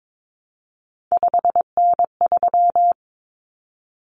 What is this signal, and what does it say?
Morse code, heard as a keyed tone, about 22 wpm, 700 Hz. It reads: 5N3
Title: Common Cut Numbers - 3-Digit 22wpm